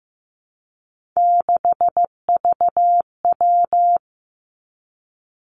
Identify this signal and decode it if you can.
Morse code, heard as a keyed tone, about 15 wpm, 700 Hz. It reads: 6VW
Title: Three Challenging Letter-Number Combinations 15wpm